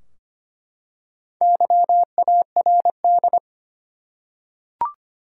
Morse code, heard as a keyed tone, about 25 wpm, 700 Hz. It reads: YARD
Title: Four Letter Words - 25wpm